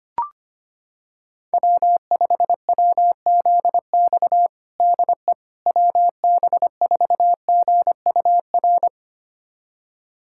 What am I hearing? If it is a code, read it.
Morse code, heard as a keyed tone, about 25 wpm, 700 Hz. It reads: W5WZX DE WB4GUR
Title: USA Callsign DE Callsign 25wpm